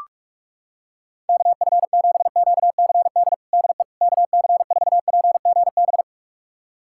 Morse code, heard as a keyed tone, about 42 wpm, 700 Hz. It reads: KF7QCD DE KC4PGB
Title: USA Callsign DE Callsign 45wpm